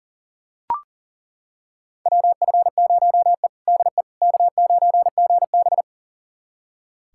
Morse code, heard as a keyed tone, about 40 wpm, 700 Hz. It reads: WF0E DE K9GB
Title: USA Callsign DE Callsign - 4 Characters - 40wpm